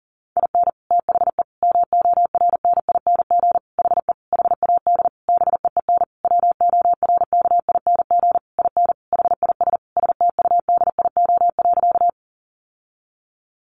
Morse code, heard as a keyed tone, about 40 wpm, 700 Hz. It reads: IN THE MORNING HE HAD BEEN WORKING IN HIS STUDIO.
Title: Sentences from Top 1200 Words 40wpm